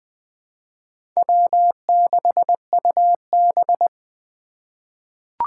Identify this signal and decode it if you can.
Morse code, heard as a keyed tone, about 20 wpm, 700 Hz. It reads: W6UB
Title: 2023 SOTA Chaser Call Signs - 20wpm